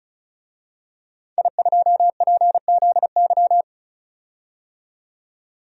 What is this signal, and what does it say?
Morse code, heard as a keyed tone, about 36 wpm, 700 Hz. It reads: I2PZY
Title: DXCC-100 Call Signs - 5 Characters - 35wpm